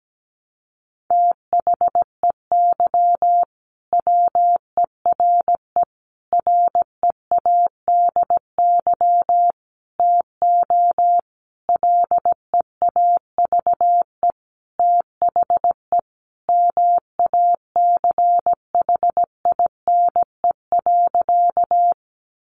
Morse code, heard as a keyed tone, about 17 wpm, 700 Hz. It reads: THEY WERE READY TO LEAVE THE MACHINE.
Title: A Sound Of Thunder 17wpm